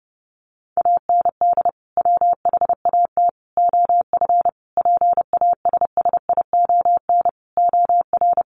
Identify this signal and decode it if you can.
Morse code, heard as a keyed tone, about 30 wpm, 700 Hz. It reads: AND WHAT OF PASSION OR
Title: Sentences from Top 1400 Words 30wpm